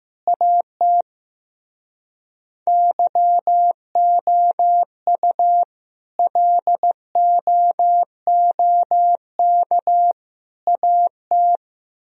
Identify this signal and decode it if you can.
Morse code, heard as a keyed tone, about 15 wpm, 700 Hz. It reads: AT YOU LOOK AT
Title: Speed-Racing - Sets of 3 Words 15wpm